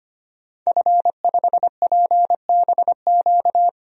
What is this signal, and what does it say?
Morse code, heard as a keyed tone, about 25 wpm, 700 Hz. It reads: F5PBQ
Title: DXCC-100 Call Signs - 5 Characters - 25wpm